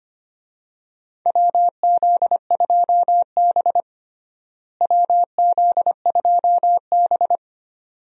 Morse code, heard as a keyed tone, about 25 wpm, 700 Hz. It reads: WZ2B WZ2B
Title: USA Call Signs - 4 Characters - Repeated 2x - 25wpm